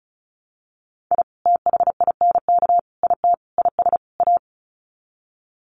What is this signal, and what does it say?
Morse code, heard as a keyed tone, about 34 wpm, 700 Hz. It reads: I THINK IT IS A
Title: Sets of 5 Words Encore 35wpm